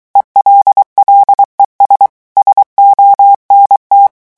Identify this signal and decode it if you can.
Morse code, heard as a keyed tone, about 23 wpm, 800 Hz. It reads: ELLES SONT
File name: morse mp3 / revision9 citations humoristiques vitesse 22 mots minute